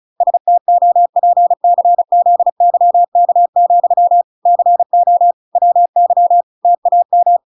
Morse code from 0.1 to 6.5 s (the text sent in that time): STOPCZYK, CO WY